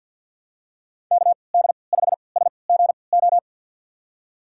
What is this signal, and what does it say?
Morse code, heard as a keyed tone, about 51 wpm, spaced as for 28 wpm, 700 Hz. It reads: KD5SGO
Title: USA Call Signs - 6 Characters - 3x Character Spacing - 50wpm